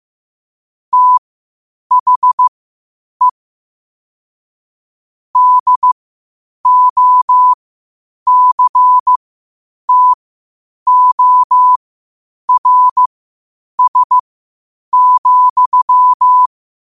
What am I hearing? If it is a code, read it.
Morse code, heard as a keyed tone, about 15 wpm, spaced as for 8 wpm, 1000 Hz. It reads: THE DOCTORS,
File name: qotdfiles / 8wpm